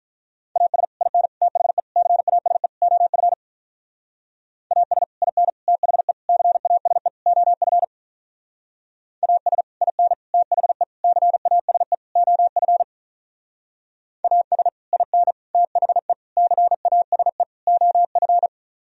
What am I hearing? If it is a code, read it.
Morse code, heard as a keyed tone, about 42 wpm, 700 Hz. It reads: AS IN THE CASE OF AS IN THE CASE OF AS IN THE CASE OF AS IN THE CASE OF
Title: Speed-Racing - Sets of 5 Words 35wpm